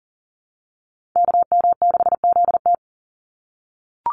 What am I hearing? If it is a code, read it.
Morse code, heard as a keyed tone, about 40 wpm, 700 Hz. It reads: KM6ZT
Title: Call Signs - 40wpm